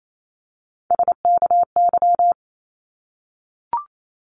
Morse code, heard as a keyed tone, about 28 wpm, 700 Hz. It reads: SKY
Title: Multi-letter Phonemes 28wpm